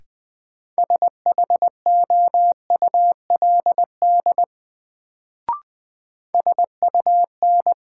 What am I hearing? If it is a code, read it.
Morse code, heard as a keyed tone, about 20 wpm, 700 Hz. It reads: SHOULD SUN
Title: Sentences from Top 300 Words - New Word Drill - 20wpm